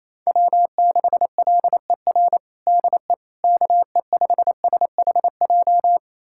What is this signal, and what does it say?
Morse code, heard as a keyed tone, about 28 wpm, 700 Hz. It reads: W6LER DE KE5SHJ
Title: USA Callsign DE Callsign 28wpm